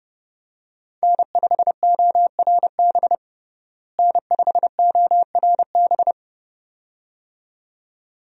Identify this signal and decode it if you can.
Morse code, heard as a keyed tone, about 32 wpm, 700 Hz. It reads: N5ORB N5ORB
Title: USA Call Signs - 5 Characters - Repeated 2x - 30wpm